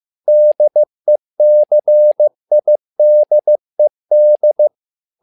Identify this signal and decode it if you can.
Morse code, heard as a keyed tone, about 15 wpm, 600 Hz. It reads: DECIDED
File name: learning center / tower trouble / tower trouble 15wpm